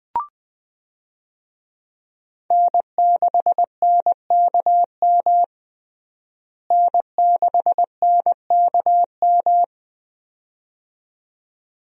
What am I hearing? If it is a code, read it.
Morse code, heard as a keyed tone, about 20 wpm, 700 Hz. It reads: N6NKM N6NKM
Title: USA Call Signs - 5 Characters - Repeated 2x - 20wpm